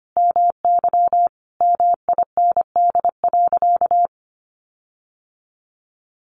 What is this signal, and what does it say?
Morse code, heard as a keyed tone, about 25 wpm, 700 Hz. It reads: MY MIND.
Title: Sentences from Top 1300 Words 25wpm